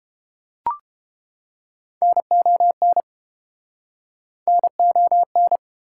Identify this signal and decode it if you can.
Morse code, heard as a keyed tone, about 32 wpm, 700 Hz. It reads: NON NON